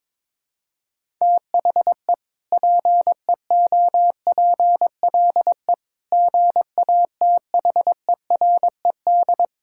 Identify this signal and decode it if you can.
Morse code, heard as a keyed tone, about 22 wpm, 700 Hz. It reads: THE PEOPLE GATHERED
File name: practice / 22wpm / Sentences with 2-Word Phrases - 22wpm